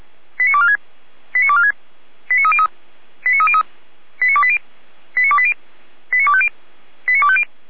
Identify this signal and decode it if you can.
Five-tone selective call (ZVEI), heard as quick runs of five tones, tones of about 70 ms. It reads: 89246 89246 89303 89303 89280 89280 89250 89250